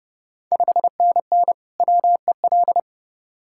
Morse code, heard as a keyed tone, about 30 wpm, 690 Hz. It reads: 5NN WEL
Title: Ontario QSO Party Exchange 30wpm